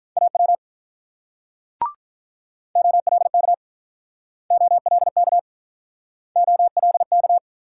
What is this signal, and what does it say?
Morse code, heard as a keyed tone, about 47 wpm, 690 Hz. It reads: AW OLK OLK OLK